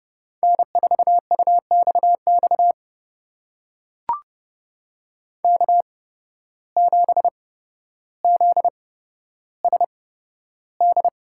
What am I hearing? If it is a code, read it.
Morse code, heard as a keyed tone, about 29 wpm, spaced as for 13 wpm, 700 Hz. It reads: N4UXX K7ZSD
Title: USA Call Signs - 5 Characters - 8x Character Spacing - 30wpm